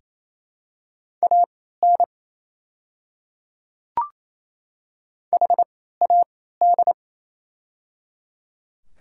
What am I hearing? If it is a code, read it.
Morse code, heard as a keyed tone, about 28 wpm, spaced as for 16 wpm, 690 Hz. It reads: AN HAD
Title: 3x Character Spacing - Top 100 Words - 28wpm